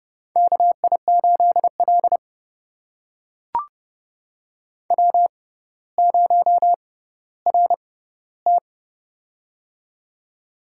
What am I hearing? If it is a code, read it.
Morse code, heard as a keyed tone, about 32 wpm, spaced as for 16 wpm, 700 Hz. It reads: KI8L W0RT
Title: USA Call Signs - 4 Characters - 6x Character Spacing - 30wpm